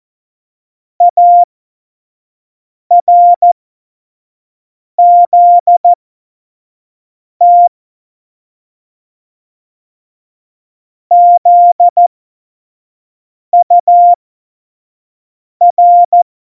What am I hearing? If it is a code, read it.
Morse code, heard as a keyed tone, about 14 wpm, spaced as for 5 wpm, 700 Hz. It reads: ARZT ZUR
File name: Morsekurs / Witze / Witz 13 March 2018, 25 F70